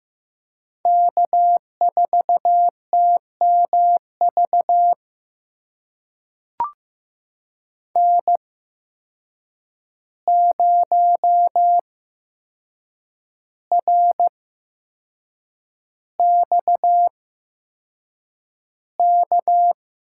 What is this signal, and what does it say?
Morse code, heard as a keyed tone, about 15 wpm, spaced as for 6 wpm, 700 Hz. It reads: K4TMV N0RXK
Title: USA Call Signs - 5 Characters - 8x Character Spacing - 15wpm